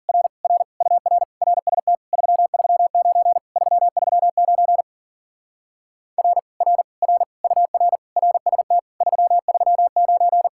Morse code, heard as a keyed tone, about 49 wpm, 700 Hz. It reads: R R UR RST 339 339 R R R UR RST 339
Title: Speed-Racing Common QSO Phrases - 40wpm